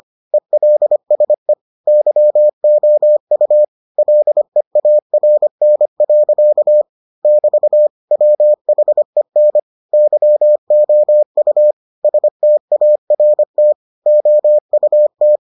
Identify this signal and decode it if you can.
Morse code, heard as a keyed tone, about 25 wpm, 600 Hz. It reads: ELSE YOU LEARN. <BT> WHEN YOU START OUT